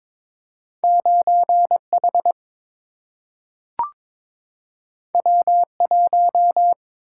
Morse code, heard as a keyed tone, about 22 wpm, 700 Hz. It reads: 9 H W 1
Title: Single-Letter Single-Number 22wpm